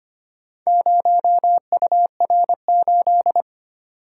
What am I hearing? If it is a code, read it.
Morse code, heard as a keyed tone, about 25 wpm, 700 Hz. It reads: 0UR8
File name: practice / 25wpm / Four Letter-Number Combinations - 25wpm